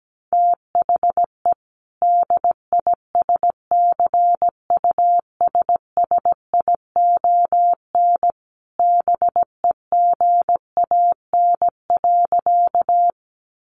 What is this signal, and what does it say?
Morse code, heard as a keyed tone, about 17 wpm, 700 Hz. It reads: THE DISCUSSION BEGAN.
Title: Sentences from Top 1100 Words 17wpm